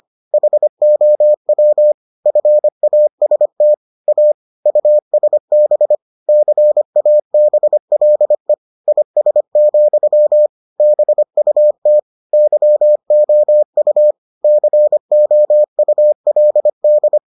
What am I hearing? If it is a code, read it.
Morse code, heard as a keyed tone, about 25 wpm, 600 Hz. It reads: HOW FAST A USB CABLE IS, BUT YOU COULD